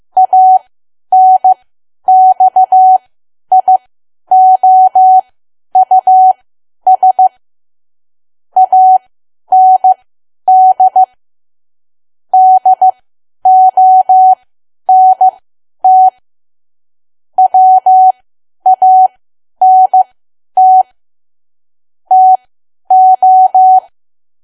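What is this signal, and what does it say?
Morse code, heard as a keyed tone, about 15 wpm, spaced as for 10 wpm, 750 Hz. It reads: ANXIOUS AND DONT WANT TO
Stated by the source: Morse code at 10 WPM for learning Morse code.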